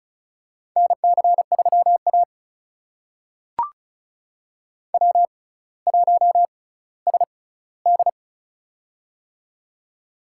Morse code, heard as a keyed tone, about 36 wpm, spaced as for 18 wpm, 700 Hz. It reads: NC3A W1SD
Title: USA Call Signs - 4 Characters - 6x Character Spacing - 35wpm